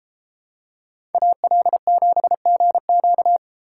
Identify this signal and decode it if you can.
Morse code, heard as a keyed tone, about 31 wpm, 700 Hz. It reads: AL7GQ